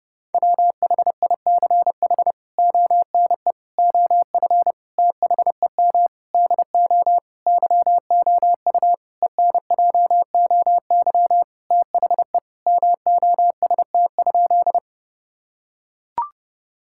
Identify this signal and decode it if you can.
Morse code, heard as a keyed tone, about 32 wpm, 700 Hz. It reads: WHICH ONE OF THEM DO YOU ENJOY THE MOST?
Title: Sentences from Top 1000 Words - Review Part 5 - 30wpm